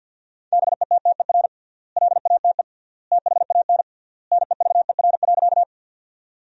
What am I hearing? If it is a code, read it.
Morse code, heard as a keyed tone, about 53 wpm, 700 Hz. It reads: BETTER LATE THAN NEVER.
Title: 3x Word-Spacing - Sentences from Top 400 Words - 50wpm